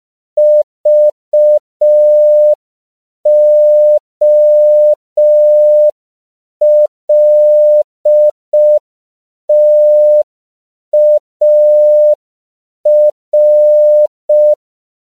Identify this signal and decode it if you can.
Morse code, heard as a keyed tone, about 5 wpm, 600 Hz. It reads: VOLTAR
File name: 5PPM